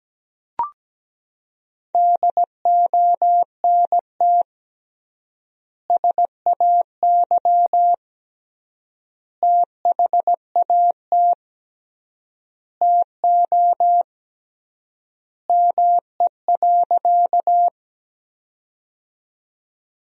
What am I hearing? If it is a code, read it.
Morse code, heard as a keyed tone, about 17 wpm, 700 Hz. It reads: DONT SAY THAT TO ME.
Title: 3x Word-Spacing - Sentences from Top 300 Words - 17wpm